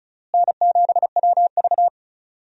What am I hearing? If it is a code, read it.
Morse code, heard as a keyed tone, about 34 wpm, 700 Hz. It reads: N7WV
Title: USA Call Signs - 4 Characters - Repeated 2x - 35wpm